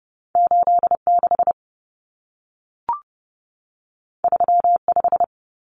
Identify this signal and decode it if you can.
Morse code, heard as a keyed tone, about 30 wpm, 700 Hz. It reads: 86 35
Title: Common Cut Numbers - 2-Digit 30wpm